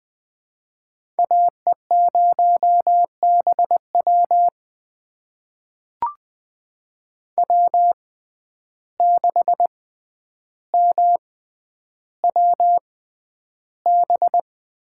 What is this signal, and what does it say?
Morse code, heard as a keyed tone, about 20 wpm, spaced as for 10 wpm, 700 Hz. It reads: AE0BW W6MWB
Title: USA Call Signs - 5 Characters - 6x Character Spacing - 20wpm